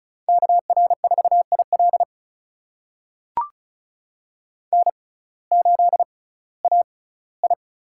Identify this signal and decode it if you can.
Morse code, heard as a keyed tone, about 36 wpm, spaced as for 19 wpm, 700 Hz. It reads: KR4IL N8AI
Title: USA Call Signs - 5 Characters - 6x Character Spacing - 35wpm